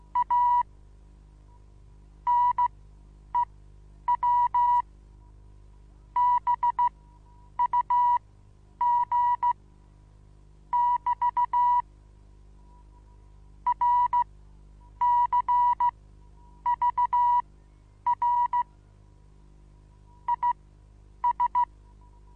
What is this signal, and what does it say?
Morse code, heard as a keyed tone, about 15 wpm, spaced as for 9 wpm, 970 Hz. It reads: A NEW BUG <BT> RCVR IS
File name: Onlinelehrgang / morsen / DJ4UF Morsen25